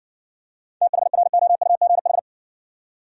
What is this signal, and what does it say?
Morse code, heard as a keyed tone, about 60 wpm, 700 Hz. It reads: THROUGH